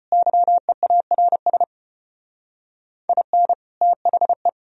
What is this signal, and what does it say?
Morse code, heard as a keyed tone, about 35 wpm, 700 Hz. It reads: YEARS IN THE